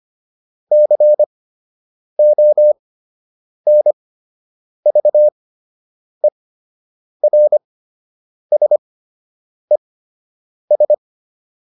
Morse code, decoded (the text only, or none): CONVERSES